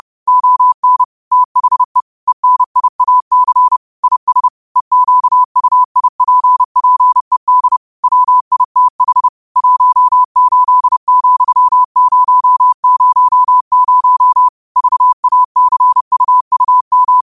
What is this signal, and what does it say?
Morse code, heard as a keyed tone, about 30 wpm, 1000 Hz. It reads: ON THE ENIAC IS EQUIPPED WITH 18,000 VACUUM
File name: qotdfiles / 30wpm